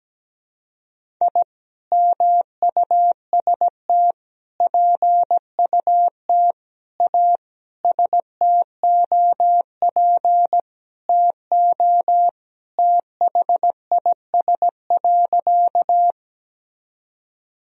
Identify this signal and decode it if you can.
Morse code, heard as a keyed tone, about 17 wpm, 700 Hz. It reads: I MUST PUT A STOP TO THIS.
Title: Sentences from Top 1000 Words - Review Part 1 - 17wpm